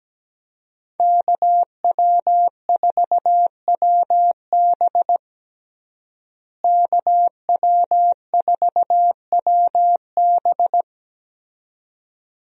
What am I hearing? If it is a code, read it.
Morse code, heard as a keyed tone, about 17 wpm, 700 Hz. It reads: KW4WB KW4WB
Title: USA Call Signs - 5 Characters - Repeated 2x - 17wpm